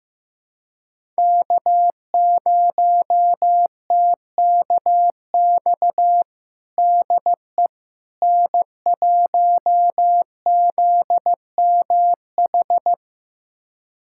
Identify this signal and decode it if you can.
Morse code, heard as a keyed tone, about 15 wpm, 700 Hz. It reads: K0TKX DE N1ZMH
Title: USA Callsign DE Callsign - 5 Characters - 15wpm